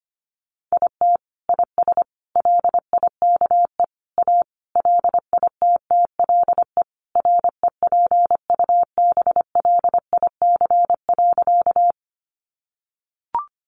Morse code, heard as a keyed tone, about 25 wpm, 700 Hz. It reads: IT IS LIKE A LITTLE REPUBLIC.
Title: Sentences from Top 1500 Words 25wpm